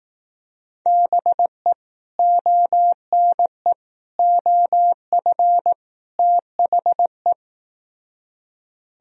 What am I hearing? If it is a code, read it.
Morse code, heard as a keyed tone, about 18 wpm, 700 Hz. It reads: BE ONE OF THE
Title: Speed-Racing - Sets of 4 Words 15wpm